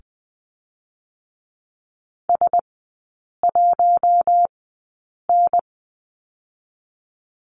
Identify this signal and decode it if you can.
Morse code, heard as a keyed tone, about 20 wpm, spaced as for 8 wpm, 700 Hz. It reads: S1N
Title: 041 Lesson 8e - Number 1 - Three Characters 20wpm